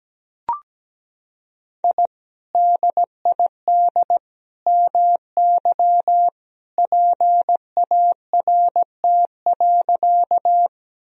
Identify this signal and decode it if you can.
Morse code, heard as a keyed tone, about 17 wpm, 700 Hz. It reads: I DID MY PART.
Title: Sentences from Top 1000 Words - Review Part 1 - 17wpm